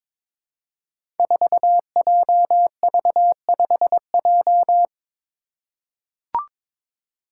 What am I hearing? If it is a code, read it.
Morse code, heard as a keyed tone, about 22 wpm, 700 Hz. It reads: 4JV5J
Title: Five Challenging Letter-Number Combinations 22wpm